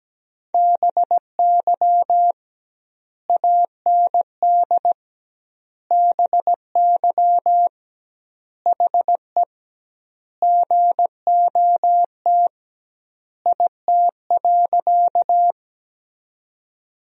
Morse code, heard as a keyed tone, about 17 wpm, 700 Hz. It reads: BY AND BY HE GOT IT.
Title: 2x Word-Spacing - Sentences from Top 300 Words - 17wpm